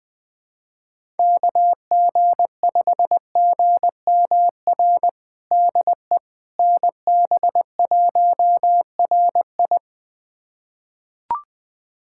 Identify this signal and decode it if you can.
Morse code, heard as a keyed tone, about 20 wpm, 700 Hz. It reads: KG5GMR DE NB1RI
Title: USA Callsign DE Callsign 20wpm